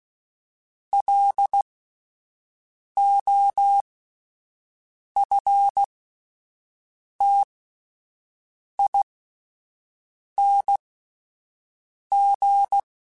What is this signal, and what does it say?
Morse code, heard as a keyed tone, about 16 wpm, spaced as for 5 wpm, 800 Hz. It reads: LOFTING
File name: Alphabetical / Alphabetical mp3s / Ls @5